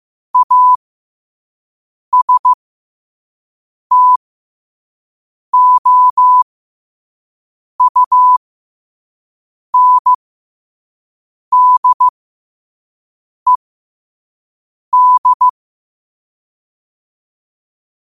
Morse code, heard as a keyed tone, about 15 wpm, spaced as for 5 wpm, 1000 Hz. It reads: ASTOUNDED
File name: qotdfiles / 5wpm